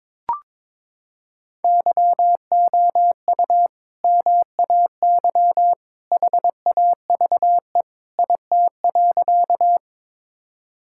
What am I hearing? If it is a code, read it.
Morse code, heard as a keyed tone, about 22 wpm, 700 Hz. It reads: YOU MAY HAVE IT.
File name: practice / 22wpm / Sentences with 2-Word Phrases - 22wpm